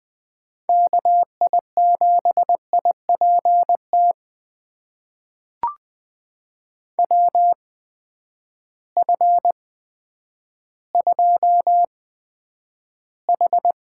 Morse code, heard as a keyed tone, about 20 wpm, spaced as for 10 wpm, 700 Hz. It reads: KI7IPT WF2H
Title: USA Call Signs - 8x Character Spacing - 20wpm